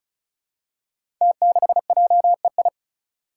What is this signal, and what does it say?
Morse code, heard as a keyed tone, about 36 wpm, 700 Hz. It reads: T6JEI
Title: Five Challenging Letter-Number Combinations 35wpm